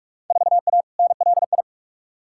Morse code, heard as a keyed tone, about 41 wpm, 700 Hz. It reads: 4A NLI
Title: Field Day Exchanges Encore - 45wpm